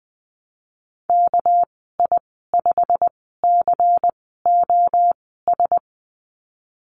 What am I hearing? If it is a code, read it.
Morse code, heard as a keyed tone, about 20 wpm, spaced as for 15 wpm, 700 Hz. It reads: KI5COS
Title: 200 Lesson 32g - Letter k - Call Signs 20wpm